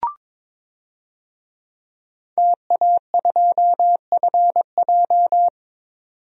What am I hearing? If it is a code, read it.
Morse code, heard as a keyed tone, about 22 wpm, 700 Hz. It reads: TA2FJ
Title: DXCC-100 Call Signs - 5 Characters - 22wpm